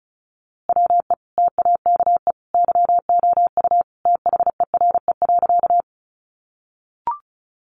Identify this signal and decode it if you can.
Morse code, heard as a keyed tone, about 34 wpm, 700 Hz. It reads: WE TAKE YOU THERE.
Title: A Sound Of Thunder 35wpm